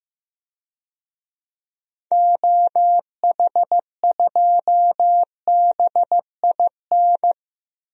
Morse code, heard as a keyed tone, about 15 wpm, 700 Hz. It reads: OH2BIN
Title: DXCC-100 Call Signs - 6 Characters - Repeated 3x - 15wpm